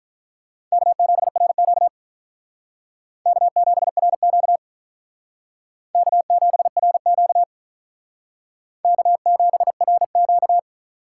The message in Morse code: K7RQ K7RQ K7RQ K7RQ

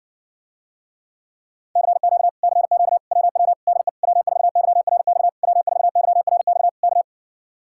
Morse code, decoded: CQ CQ WW DE W4YRX W4YRX K

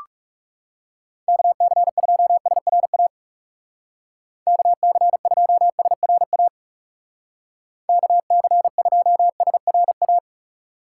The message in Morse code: KC2SRA KC2SRA KC2SRA

